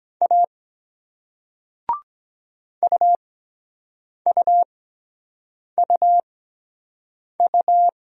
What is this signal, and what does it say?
Morse code, heard as a keyed tone, about 23 wpm, spaced as for 7 wpm, 700 Hz. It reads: A UUUU